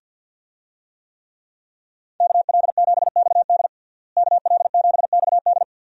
Morse code, heard as a keyed tone, about 52 wpm, 700 Hz. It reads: KL7XD KL7XD
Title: DXCC-100 Call Signs - 5 Characters - Repeated 3x - 50wpm